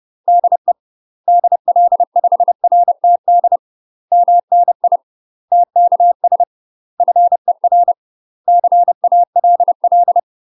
Morse code, heard as a keyed tone, about 30 wpm, 700 Hz. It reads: DE DL5RTD MNI TKS FER CALL
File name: UK MASTER QSOs TRAINING / 30WPM / UK MASTER QSOs 0005